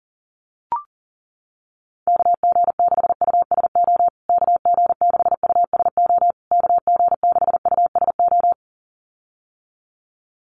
Morse code, heard as a keyed tone, about 40 wpm, 700 Hz. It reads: KG6USO KG6USO KG6USO
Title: USA Call Signs Repeated 3x 40wpm